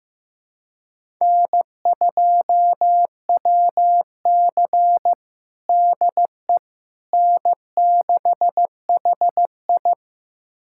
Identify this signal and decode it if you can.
Morse code, heard as a keyed tone, about 15 wpm, 700 Hz. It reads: N2WC DE N6HI
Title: USA Callsign DE Callsign - 4 Characters - 15wpm